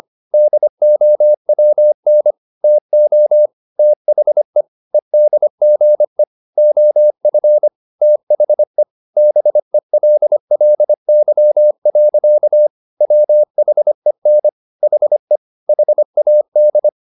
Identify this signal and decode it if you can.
Morse code, heard as a keyed tone, about 25 wpm, 600 Hz. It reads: DOWN TO THE EDGE OF THE BELLY. WHEN HE HAD
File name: the old man and the sea part 1 25wpm